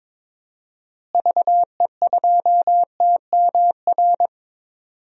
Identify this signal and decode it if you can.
Morse code, heard as a keyed tone, about 22 wpm, 700 Hz. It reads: VE2TMR
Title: DXCC-100 Call Signs - 6 Characters - Repeated 3x - 22wpm